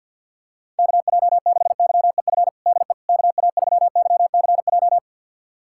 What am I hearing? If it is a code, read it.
Morse code, heard as a keyed tone, about 52 wpm, 700 Hz. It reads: KJ6YEF DE KA3YCJ
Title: USA Callsign DE Callsign 50wpm